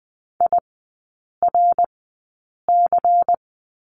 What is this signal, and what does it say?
Morse code, heard as a keyed tone, about 19 wpm, spaced as for 8 wpm, 700 Hz. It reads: IRC
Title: 100 Lesson 17e - Letter u - Three Characters 20wpm